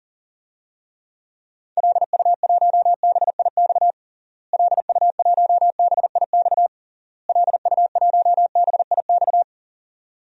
Morse code, heard as a keyed tone, about 40 wpm, 700 Hz. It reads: LU1BIX LU1BIX LU1BIX
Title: DXCC-100 Call Signs - 6 Characters - Repeated 3x - 40wpm